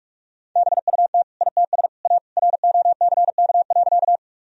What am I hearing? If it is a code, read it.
Morse code, heard as a keyed tone, about 41 wpm, 700 Hz. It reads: BUT ITS A ROCK.
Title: Sentences from Top 1000 Words - Review Part 2 - 45wpm